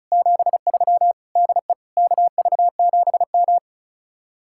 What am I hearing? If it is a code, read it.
Morse code, heard as a keyed tone, about 45 wpm, 700 Hz. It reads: 73 DE KV7M